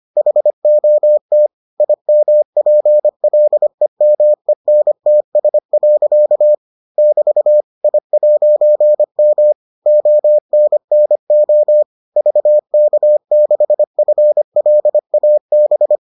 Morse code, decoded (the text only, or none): HOT IMPLEMENTS. <BT> I'M ONNO VK6FLAB